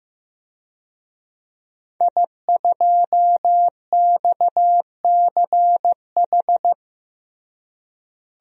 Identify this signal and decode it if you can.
Morse code, heard as a keyed tone, about 15 wpm, 700 Hz. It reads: I2XCH
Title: DXCC-100 Call Signs - 5 Characters - Repeated 3x - 15wpm